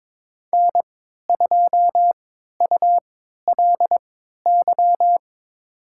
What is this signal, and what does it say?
Morse code, heard as a keyed tone, about 22 wpm, spaced as for 12 wpm, 700 Hz. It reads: N2ULY
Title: USA Call Signs - 5 Characters - 3x Character Spacing - 22wpm